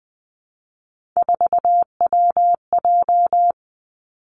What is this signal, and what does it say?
Morse code, heard as a keyed tone, about 20 wpm, 700 Hz. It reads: 4WJ
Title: Three Challenging Letter-Number Combinations 20wpm